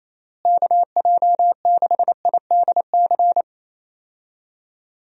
Morse code, heard as a keyed tone, about 28 wpm, 700 Hz. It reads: KJ6IDC
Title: USA Call Signs - 6 Characters - Repeated 2x - 28wpm